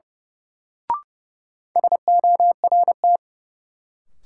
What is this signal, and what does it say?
Morse code, heard as a keyed tone, about 31 wpm, 700 Hz. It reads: SORT
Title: Sentences from Top 1300 Words - New Word Drill Alt - 30wpm